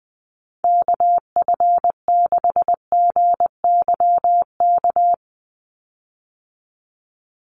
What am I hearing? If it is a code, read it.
Morse code, heard as a keyed tone, about 20 wpm, 700 Hz. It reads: KF6GYK
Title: USA Call Signs 20wpm